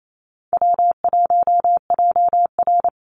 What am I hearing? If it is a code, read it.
Morse code, heard as a keyed tone, about 28 wpm, 700 Hz. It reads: W1JR
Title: Easy Call Signs - 28wpm